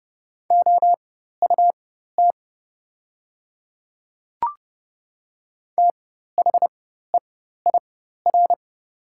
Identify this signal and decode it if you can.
Morse code, heard as a keyed tone, about 30 wpm, spaced as for 14 wpm, 700 Hz. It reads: OUT THEIR
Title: 4x Character Spacing - Top 100 Words - 30wpm